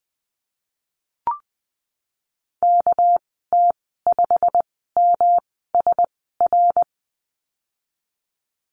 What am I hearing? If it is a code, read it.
Morse code, heard as a keyed tone, about 20 wpm, spaced as for 15 wpm, 700 Hz. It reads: KT5MSR
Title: 231 Lesson 37g - Letter x - Call Signs 20wpm